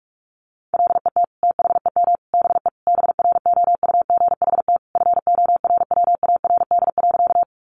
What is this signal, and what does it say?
Morse code, heard as a keyed tone, about 44 wpm, 700 Hz. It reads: LET THEM BE BROUGHT FORWARD.
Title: Sentences from Top 1300 Words 45wpm